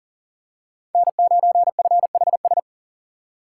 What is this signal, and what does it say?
Morse code, heard as a keyed tone, about 40 wpm, 700 Hz. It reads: N9FHS
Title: USA Call Signs - 5 Characters - 4x Character Spacing - 40wpm